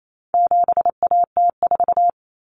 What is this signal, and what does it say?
Morse code, heard as a keyed tone, about 28 wpm, 700 Hz. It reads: 7AT4
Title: Common Cut Numbers - 4-Digit 28wpm